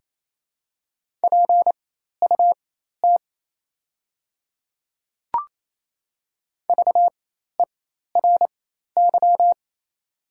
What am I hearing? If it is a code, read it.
Morse code, heard as a keyed tone, about 28 wpm, spaced as for 13 wpm, 700 Hz. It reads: PUT VERY
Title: 4x Character Spacing - Top 200 Words - 28wpm